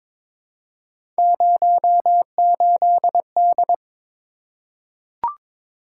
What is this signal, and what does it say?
Morse code, heard as a keyed tone, about 22 wpm, 700 Hz. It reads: 08D
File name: practice / 22wpm / Three Letter-Number Combinations - 22wpm